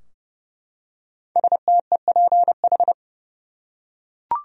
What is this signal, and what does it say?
Morse code, heard as a keyed tone, about 31 wpm, 700 Hz. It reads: STEPH
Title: USA Common Names - 30wpm